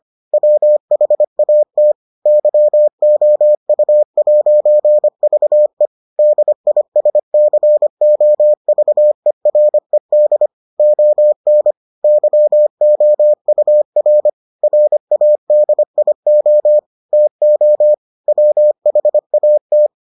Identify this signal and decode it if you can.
Morse code, heard as a keyed tone, about 25 wpm, 600 Hz. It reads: WHAT YOU'VE DISCOVERED ON YOUR RADIO TO WHAT